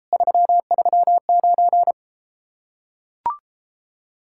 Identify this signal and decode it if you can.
Morse code, heard as a keyed tone, about 33 wpm, 700 Hz. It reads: 339